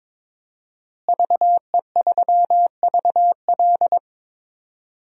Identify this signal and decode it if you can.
Morse code, heard as a keyed tone, about 22 wpm, 700 Hz. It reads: VE3VL
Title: DXCC-100 Call Signs - 5 Characters - Repeated 2x - 22wpm